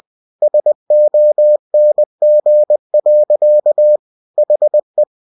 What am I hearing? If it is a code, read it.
Morse code, heard as a keyed tone, about 20 wpm, 600 Hz. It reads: SONG. HE
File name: cw-books / winnie-the-pooh / winnie-20wpm-page-0003